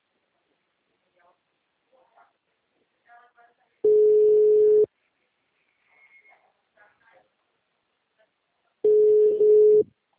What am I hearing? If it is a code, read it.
Ringback tone (Europe, 425 Hz), with speech.
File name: out- -2024- - - .37253